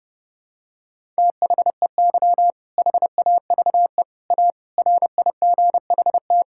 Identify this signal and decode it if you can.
Morse code, heard as a keyed tone, about 31 wpm, 700 Hz. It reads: THEY HAVE A RIGHT
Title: Sentences from Top 1000 Words - Review Part 5 - 30wpm